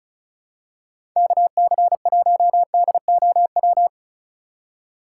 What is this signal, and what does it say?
Morse code, heard as a keyed tone, about 36 wpm, 700 Hz. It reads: KC1DOW
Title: USA Call Signs - 6 Characters - 3x Character Spacing - 35wpm